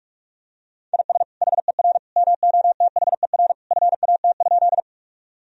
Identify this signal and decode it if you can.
Morse code, heard as a keyed tone, about 46 wpm, 700 Hz. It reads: IS HER MOTHER FAT?
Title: Sentences from Top 1000 Words - Review Part 4 - 45wpm